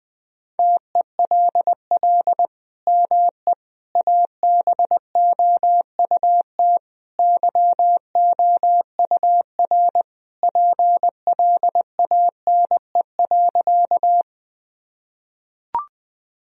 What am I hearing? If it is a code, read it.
Morse code, heard as a keyed tone, about 21 wpm, 700 Hz. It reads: TELL ME ABOUT YOUR PLANE.
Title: Sentences from Top 1000 Words - Review Part 3 - 20wpm